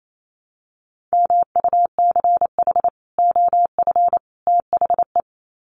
Morse code, heard as a keyed tone, about 28 wpm, 700 Hz. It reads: MUCH OF THE
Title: Sets of 3 Words 28wpm